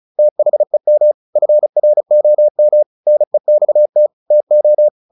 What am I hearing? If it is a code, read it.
Morse code, heard as a keyed tone, about 34 wpm, 600 Hz. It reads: THEM FROM NEXT TO
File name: the old man and the sea / the old man and the sea 1 35wpm